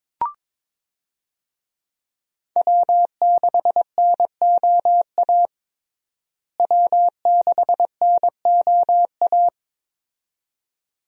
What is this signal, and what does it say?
Morse code, heard as a keyed tone, about 22 wpm, 700 Hz. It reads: W6NOA W6NOA
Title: USA Call Signs - 5 Characters - Repeated 2x - 22wpm